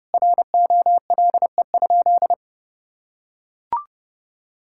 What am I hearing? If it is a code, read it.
Morse code, heard as a keyed tone, about 30 wpm, 700 Hz. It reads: ROLE?
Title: Sentences from Top 1000 Words - Review Part 6 - 30wpm